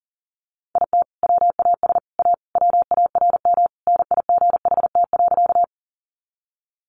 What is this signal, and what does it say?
Morse code, heard as a keyed tone, about 40 wpm, 700 Hz. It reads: IT WAS A WARM NIGHT.
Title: Sentences from Top 500 Words 40wpm